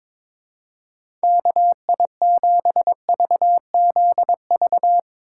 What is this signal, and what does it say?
Morse code, heard as a keyed tone, about 22 wpm, 700 Hz. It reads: KI7VZV
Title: USA Call Signs - 6 Characters - 3x Character Spacing - 22wpm